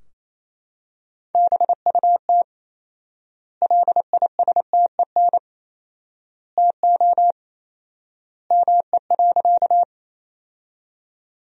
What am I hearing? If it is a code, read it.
Morse code, heard as a keyed tone, about 27 wpm, 700 Hz. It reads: BUT LISTEN TO ME.
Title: 4x Word-Spacing - Sentences from Top 400 Words - 28wpm